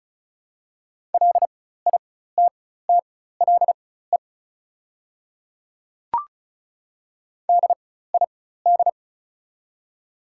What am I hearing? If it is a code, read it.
Morse code, heard as a keyed tone, about 36 wpm, spaced as for 16 wpm, 690 Hz. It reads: LITTLE DID
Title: 4x Character Spacing - Top 500 Words - 35wpm